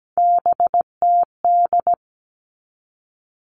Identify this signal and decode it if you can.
Morse code, heard as a keyed tone, about 17 wpm, 700 Hz. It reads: BTD
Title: All Cut Numbers - 3-Digit 17wpm